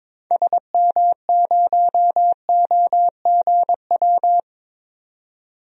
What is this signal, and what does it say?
Morse code, heard as a keyed tone, about 22 wpm, 700 Hz. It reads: SM0OGW
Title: DXCC-100 Call Signs - 6 Characters - 22wpm